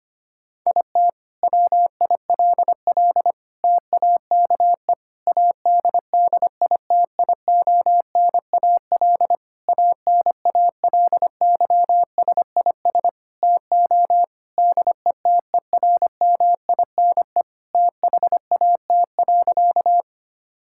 Morse code, heard as a keyed tone, about 25 wpm, 700 Hz. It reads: IT WILL TAKE ADDITIONAL ANALYSIS TO DETERMINE THAT.
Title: Sentences from Top 1000 Words - Review Part 5 - 25wpm